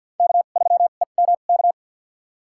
Morse code, heard as a keyed tone, about 50 wpm, spaced as for 36 wpm, 700 Hz. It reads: K3EMK
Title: USA Call Signs - 5 Characters - 2x Character Spacing - 50wpm